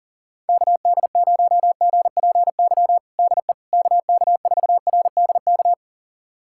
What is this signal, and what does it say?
Morse code, heard as a keyed tone, about 40 wpm, 700 Hz. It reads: KD0GPY DE KK4RDK
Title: USA Callsign DE Callsign 40wpm